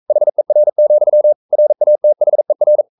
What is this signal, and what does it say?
Morse code, heard as a keyed tone, about 42 wpm, 600 Hz. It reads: HER, RATHER
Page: Romanen Never af Ken Follet på engelsk.